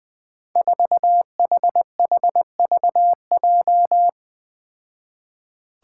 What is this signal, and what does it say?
Morse code, heard as a keyed tone, about 21 wpm, 700 Hz. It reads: 4HHVJ
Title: Five Challenging Letter-Number Combinations 20wpm